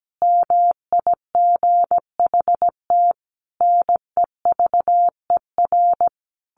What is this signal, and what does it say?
Morse code, heard as a keyed tone, about 17 wpm, 700 Hz. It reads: MIGHT NEVER
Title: A Sound Of Thunder 17wpm